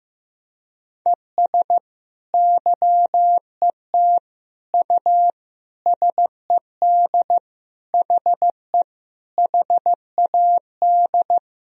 Morse code, heard as a keyed tone, about 15 wpm, 700 Hz. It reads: ES YET U SED HE HAD
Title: Sentences from Top 500 Words and Abbreviations 15wpm